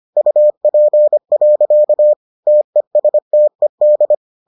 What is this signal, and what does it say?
Morse code, heard as a keyed tone, about 25 wpm, 600 Hz. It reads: UP. TESTED